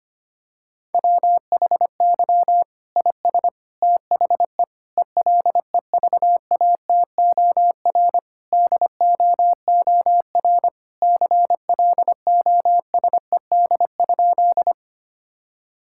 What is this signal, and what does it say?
Morse code, heard as a keyed tone, about 25 wpm, 700 Hz. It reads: WHY IS THE ELEVATOR DOOR CLOSED?
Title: Sentences from Top 1600 Words 25wpm